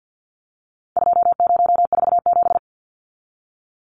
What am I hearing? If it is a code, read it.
Morse code, heard as a keyed tone, about 50 wpm, 700 Hz. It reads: 2047
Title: Numbers - 4-Digit 50wpm